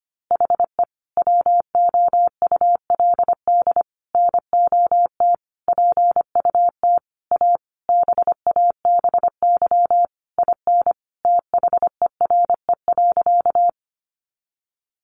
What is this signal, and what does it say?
Morse code, heard as a keyed tone, about 25 wpm, 700 Hz. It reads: HE WOULD NOT PUT A BABY IN THERE.
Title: Easy Phrases 25wpm